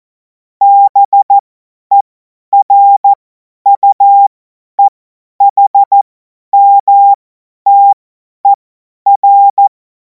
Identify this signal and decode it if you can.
Morse code, heard as a keyed tone, about 14 wpm, spaced as for 10 wpm, 800 Hz. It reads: BERUEHMTER
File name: Morsekurs / Witze / Witz 10. März 2026 50 F70